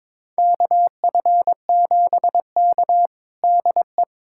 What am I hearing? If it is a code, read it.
Morse code, heard as a keyed tone, about 22 wpm, 700 Hz. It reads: KF7K DE
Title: USA Callsign DE Callsign - 4 Characters - 22wpm